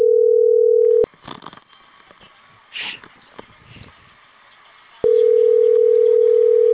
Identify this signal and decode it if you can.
Ringback tone (North America), 440 + 480 Hz.